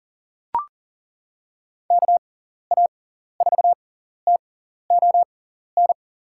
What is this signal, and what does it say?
Morse code, heard as a keyed tone, about 40 wpm, spaced as for 14 wpm, 700 Hz. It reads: KA4TON
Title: USA Call Signs - 6 Characters - 6x Character Spacing - 40wpm